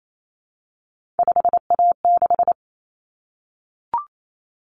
Morse code, heard as a keyed tone, about 28 wpm, 700 Hz. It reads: 5A6
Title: Common Cut Numbers - 3-Digit 28wpm